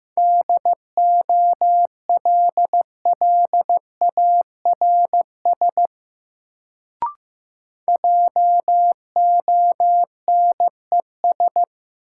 Morse code, heard as a keyed tone, about 15 wpm, 700 Hz. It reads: DOLLARS JONES
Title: Sets of 5 Words - New Word Drill - 15wpm